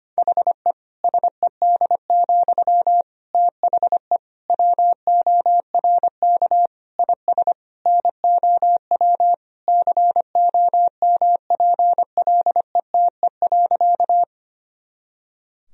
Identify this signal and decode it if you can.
Morse code, heard as a keyed tone, about 25 wpm, 700 Hz. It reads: HE SED, THE WORK IS NOW COMPLETE.
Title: Sentences from Top 400 Words and Abbreviations 25wpm